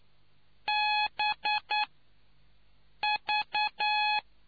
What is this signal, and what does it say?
Morse code, heard as a keyed tone, about 9 wpm, spaced as for 5 wpm, 840 Hz. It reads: BV